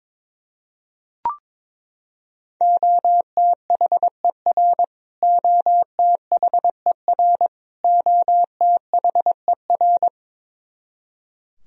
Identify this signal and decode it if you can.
Morse code, heard as a keyed tone, about 22 wpm, 700 Hz. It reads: OTHER OTHER OTHER
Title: Top 100 Words Repeated 22wpm